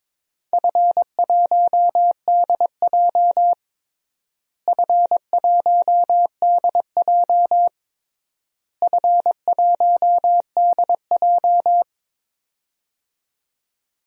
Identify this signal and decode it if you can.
Morse code, heard as a keyed tone, about 22 wpm, 700 Hz. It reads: F1DJ F1DJ F1DJ
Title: DXCC-100 Call Signs - 4 Characters - Repeated 3x - 22wpm